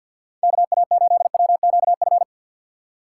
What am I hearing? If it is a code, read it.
Morse code, heard as a keyed tone, about 51 wpm, 700 Hz. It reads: KA8WQF
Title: USA Call Signs - 6 Characters - 4x Character Spacing - 50wpm